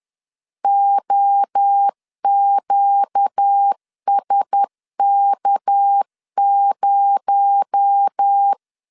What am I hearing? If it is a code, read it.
Morse code, heard as a keyed tone, about 10 wpm, 800 Hz. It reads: OQSK0
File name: LEZIONI / 6 PKX0 / 6 PKX0 10wpm 2707